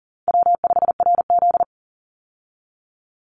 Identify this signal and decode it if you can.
Morse code, heard as a keyed tone, about 40 wpm, 700 Hz. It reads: W5RZ
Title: Easy Call Signs Repeated 2x - 40wpm